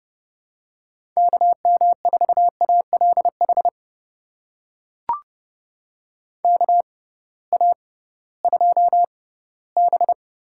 Morse code, heard as a keyed tone, about 30 wpm, spaced as for 17 wpm, 700 Hz. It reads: KM4ALH KA2B
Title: USA Call Signs - 6 Characters - 6x Character Spacing - 30wpm